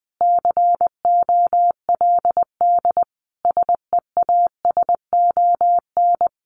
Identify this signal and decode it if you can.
Morse code, heard as a keyed tone, about 20 wpm, 700 Hz. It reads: COLD SEASON
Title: Sentences from Top 1400 Words 20wpm